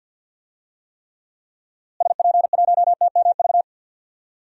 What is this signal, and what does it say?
Morse code, heard as a keyed tone, about 49 wpm, 700 Hz. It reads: SP1TMV
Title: DXCC-100 Call Signs - 6 Characters - Repeated 2x - 50wpm